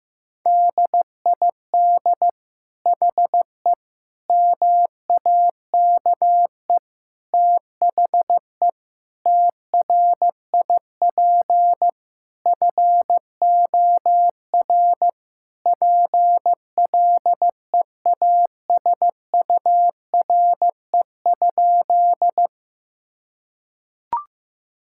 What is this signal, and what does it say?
Morse code, heard as a keyed tone, about 16 wpm, 700 Hz. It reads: DID HE MAKE THE TRIP FOR PLEASURE?
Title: Sentences from Top 1600 Words 15wpm